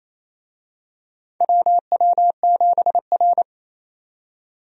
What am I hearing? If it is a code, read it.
Morse code, heard as a keyed tone, about 27 wpm, 700 Hz. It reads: WW7R
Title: USA Call Signs - 4 Characters - 28wpm